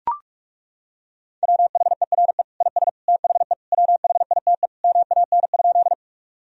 Morse code, heard as a keyed tone, about 41 wpm, 700 Hz. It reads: WHERE IS THE WHITE MAN?
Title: Sentences from Top 1000 Words - Review Part 1 - 45wpm